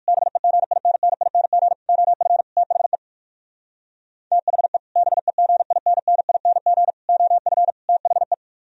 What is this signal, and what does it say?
Morse code, heard as a keyed tone, about 48 wpm, 700 Hz. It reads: BEGINNING OF THE THE BEGINNING OF THE
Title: Speed-Racing - Sets of 4 Words 45wpm